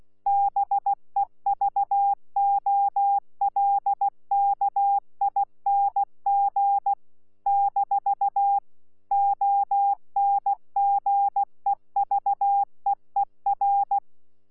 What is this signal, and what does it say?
Morse code, heard as a keyed tone, about 16 wpm, 800 Hz. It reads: BEVOLKING - ONGEVEER
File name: Z PI4VRZ / PI4VRZ 14-jun-2025